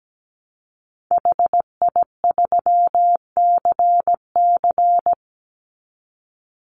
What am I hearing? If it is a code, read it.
Morse code, heard as a keyed tone, about 17 wpm, 700 Hz. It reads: HI3CC
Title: Call Signs - 17wpm